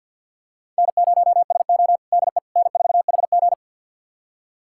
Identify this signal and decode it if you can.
Morse code, heard as a keyed tone, about 47 wpm, 700 Hz. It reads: N0SO DE N4HG
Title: USA Callsign DE Callsign - 4 Characters - 50wpm